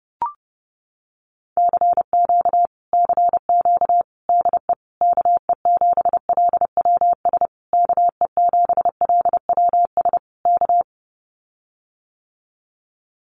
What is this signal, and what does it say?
Morse code, heard as a keyed tone, about 30 wpm, 700 Hz. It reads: CQ CQ DE KE7LWS KE7LWS K
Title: Common QSO Phrases Encore 30wpm